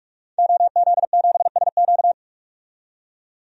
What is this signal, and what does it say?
Morse code, heard as a keyed tone, about 44 wpm, 690 Hz. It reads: OZ7SQ
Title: DXCC-100 Call Signs - 5 Characters - 45wpm